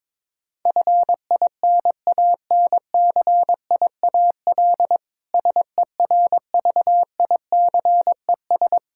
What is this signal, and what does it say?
Morse code, heard as a keyed tone, about 22 wpm, 700 Hz. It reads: FINANCIAL SERVICES
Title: K1USN SST QSO Practice 22wpm